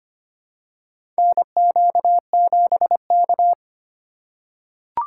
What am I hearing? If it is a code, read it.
Morse code, heard as a keyed tone, about 25 wpm, 700 Hz. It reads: NQ7K
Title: USA Call Signs - 4 Characters - 2x Character Spacing - 25wpm